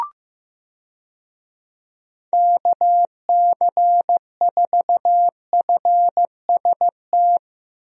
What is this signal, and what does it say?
Morse code, heard as a keyed tone, about 15 wpm, 700 Hz. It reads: KC4FST
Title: USA Call Signs - 6 Characters - Repeated 2x - 15wpm